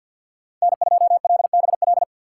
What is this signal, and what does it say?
Morse code, heard as a keyed tone, about 53 wpm, 700 Hz. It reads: N2LBL
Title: USA Call Signs - 5 Characters - 2x Character Spacing - 50wpm